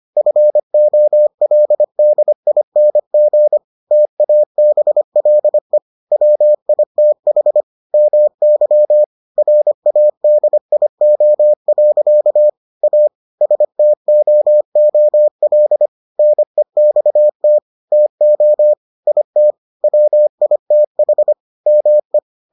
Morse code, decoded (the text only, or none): FOLDING TABLE WITH MY RADIO. A STOOL NEXT TO IT WITH ME